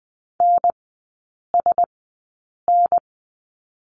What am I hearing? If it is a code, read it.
Morse code, heard as a keyed tone, about 20 wpm, spaced as for 8 wpm, 700 Hz. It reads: NSN
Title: 047 Lesson 9e - Number 4 - Three Characters 20wpm